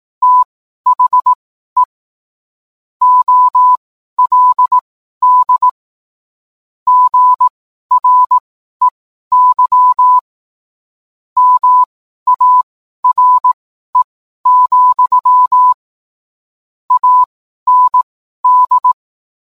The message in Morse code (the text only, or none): THE OLD GREY MARE, AND